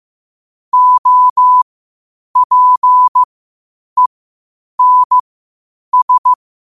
Morse code, heard as a keyed tone, about 15 wpm, spaced as for 8 wpm, 1000 Hz. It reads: OPENS
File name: qotdfiles / 8wpm